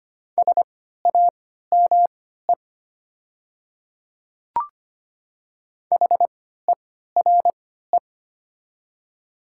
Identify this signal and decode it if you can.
Morse code, heard as a keyed tone, about 25 wpm, spaced as for 14 wpm, 700 Hz. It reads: SAME HERE
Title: 3x Character Spacing - Top 200 Words - 25wpm